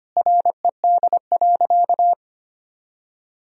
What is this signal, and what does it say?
Morse code, heard as a keyed tone, about 25 wpm, 700 Hz. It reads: RED.
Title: Sentences from Top 1000 Words - Review Part 2 - 25wpm